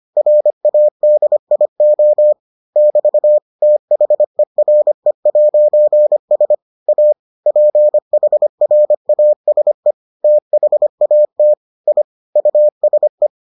Morse code, heard as a keyed tone, about 25 wpm, 600 Hz. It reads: RADIO <BT> THERE'S A PHRASE THAT I USE